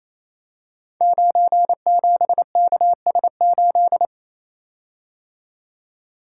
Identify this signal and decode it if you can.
Morse code, heard as a keyed tone, about 27 wpm, 700 Hz. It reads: 97KS8
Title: Code Groups - 5 Letter:Numbers - 28wpm